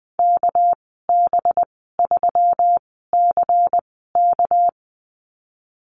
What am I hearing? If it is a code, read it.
Morse code, heard as a keyed tone, about 20 wpm, spaced as for 15 wpm, 700 Hz. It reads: KB3CK
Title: 231 Lesson 37g - Letter x - Call Signs 20wpm